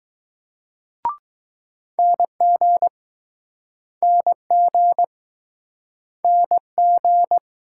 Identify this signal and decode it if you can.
Morse code, heard as a keyed tone, about 20 wpm, 700 Hz. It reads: NG NG NG